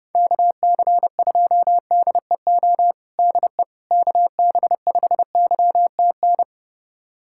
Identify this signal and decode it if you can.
Morse code, heard as a keyed tone, about 30 wpm, 700 Hz. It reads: KC2DEO DE KB5YTN
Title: USA Callsign DE Callsign 30wpm